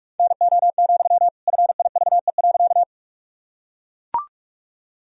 Morse code, heard as a keyed tone, about 43 wpm, 700 Hz. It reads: NO, FIVE.
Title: Sentences from Top 1000 Words - Review Part 2 - 45wpm